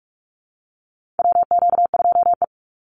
Morse code, heard as a keyed tone, about 44 wpm, 700 Hz. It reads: WQ2E
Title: Call Signs Repeated 3x - 45wpm